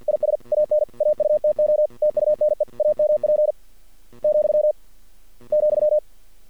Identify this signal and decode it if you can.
Morse code, heard as a keyed tone, about 50 wpm, 610 Hz. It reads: IS AN AUTO TUNE AMP , ,
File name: archived / CRSnet